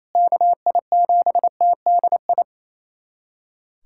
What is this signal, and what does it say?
Morse code, heard as a keyed tone, about 28 wpm, 700 Hz. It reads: KI7TDI
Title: USA Callsign DE Callsign - 6 Characters - 28wpm